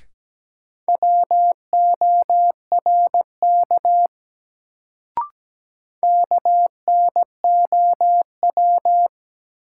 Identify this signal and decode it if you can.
Morse code, heard as a keyed tone, about 17 wpm, 700 Hz. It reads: WORK KNOW
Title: Sets of 2 Words - New Word Drill - 17wpm